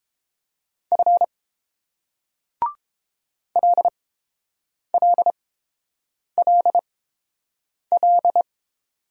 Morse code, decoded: F LLLL

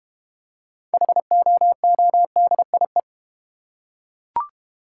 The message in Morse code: HOODIE

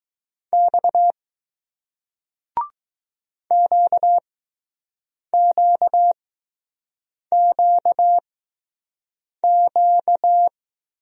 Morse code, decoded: X QQQQ

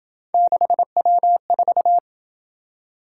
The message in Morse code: 6W4

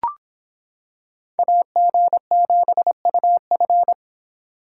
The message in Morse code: AG7UF

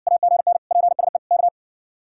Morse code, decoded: AGN PSE K

